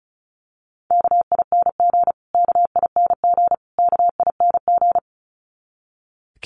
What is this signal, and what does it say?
Morse code, heard as a keyed tone, about 34 wpm, 700 Hz. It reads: KING KING KING
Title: Top 500 Words Repeated 35wpm